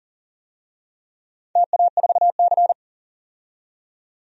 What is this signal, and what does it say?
Morse code, heard as a keyed tone, about 40 wpm, 700 Hz. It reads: TA4C
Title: DXCC-100 Call Signs - 4 Characters - 40wpm